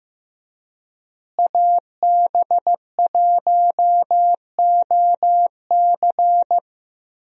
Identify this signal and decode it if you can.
Morse code, heard as a keyed tone, about 15 wpm, 700 Hz. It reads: AB1OC
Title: USA Call Signs - 5 Characters - Repeated 3x - 15wpm